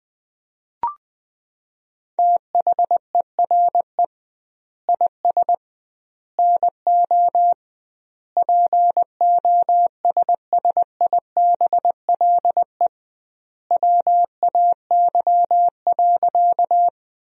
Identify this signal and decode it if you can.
Morse code, heard as a keyed tone, about 20 wpm, 700 Hz. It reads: THERE IS NO POSSIBLE WAY.
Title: 2x Word-Spacing - Sentences from Top 500 Words - 20wpm